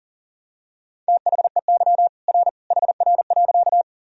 Morse code, heard as a keyed tone, about 40 wpm, 700 Hz. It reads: THEY R HR.
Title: Sentences from Top 200 Words and Abbreviations 40wpm